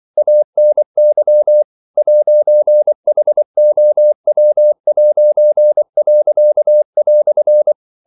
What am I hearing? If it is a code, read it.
Morse code, heard as a keyed tone, about 24 wpm, 600 Hz. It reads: ANY 'HOW'."
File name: CW / Practice / EnglishQuotations0224